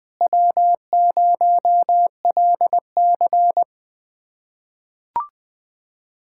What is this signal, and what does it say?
Morse code, heard as a keyed tone, about 20 wpm, 700 Hz. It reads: W0LC
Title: USA Callsign DE Callsign - 4 Characters - 20wpm